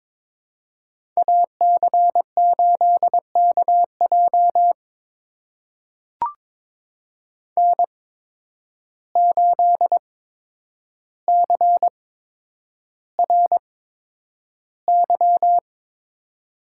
Morse code, heard as a keyed tone, about 23 wpm, spaced as for 10 wpm, 700 Hz. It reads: AC8KJ N8CRY
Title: USA Call Signs - 8x Character Spacing - 22wpm